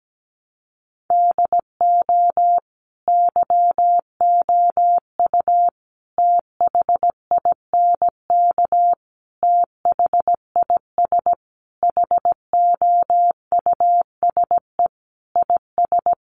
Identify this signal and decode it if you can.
Morse code, heard as a keyed tone, about 17 wpm, 700 Hz. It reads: DO YOU THINK THIS HOUSE IS
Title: Sentences from Top 1000 Words 17wpm